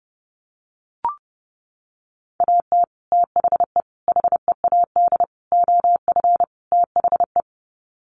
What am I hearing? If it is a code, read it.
Morse code, heard as a keyed tone, about 30 wpm, 700 Hz. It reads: AT THE HEAD OF THE
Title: Sets of 5 Words 30wpm